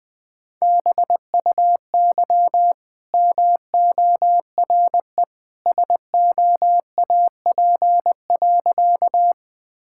Morse code, heard as a keyed tone, about 20 wpm, 700 Hz. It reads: BUY MORE SOAP.
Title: Easy Sentences with Hints - 20wpm